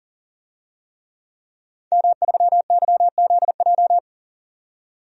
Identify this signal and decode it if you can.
Morse code, heard as a keyed tone, about 40 wpm, 700 Hz. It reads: M3YZJ
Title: DXCC-100 Call Signs - 5 Characters - 40wpm